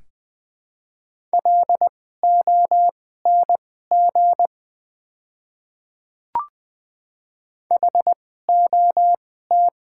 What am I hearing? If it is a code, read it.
Morse code, heard as a keyed tone, about 20 wpm, spaced as for 15 wpm, 700 Hz. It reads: LONG HOT
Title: 2x Character Spacing - Top 100 Words - 20wpm